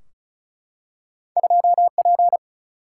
Morse code, heard as a keyed tone, about 36 wpm, 700 Hz. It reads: 2P
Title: Single-Letter Single-Number 35wpm